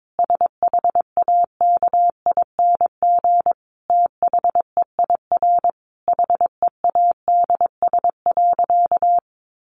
Morse code, heard as a keyed tone, about 22 wpm, 700 Hz. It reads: SHAKING THEIR HEADS.
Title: A Sound Of Thunder 22wpm